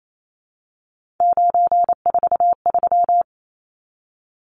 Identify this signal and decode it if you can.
Morse code, heard as a keyed tone, about 28 wpm, 700 Hz. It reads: 943
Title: Numbers - 3-Digit 28wpm